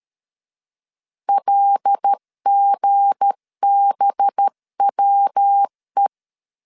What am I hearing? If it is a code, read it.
Morse code, heard as a keyed tone, about 13 wpm, 800 Hz. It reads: LGBWE
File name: LEZIONI / 7 GF9Z / 7 G 12wpm 2055